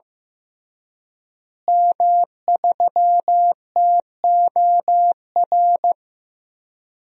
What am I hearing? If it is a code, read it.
Morse code, heard as a keyed tone, about 15 wpm, 700 Hz. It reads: M3TOR
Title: DXCC-100 Call Signs - 5 Characters - Repeated 3x - 15wpm